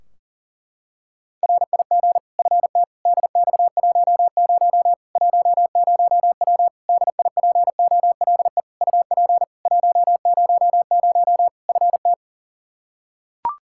Morse code, heard as a keyed tone, about 40 wpm, 700 Hz. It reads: RIG FT DX10 10W DIPOLE UP 100 FT
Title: Practice QSO - Exchange 2 - Info - 40wpm